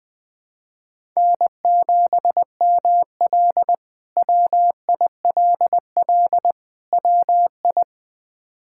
Morse code, heard as a keyed tone, about 21 wpm, 700 Hz. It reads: N7ML WILL WI
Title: K1USN SST QSO Practice - 20wpm